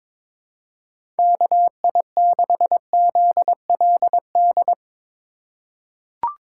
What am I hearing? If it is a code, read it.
Morse code, heard as a keyed tone, about 22 wpm, 700 Hz. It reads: KI6ZLD
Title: USA Call Signs - 6 Characters - 2x Character Spacing - 22wpm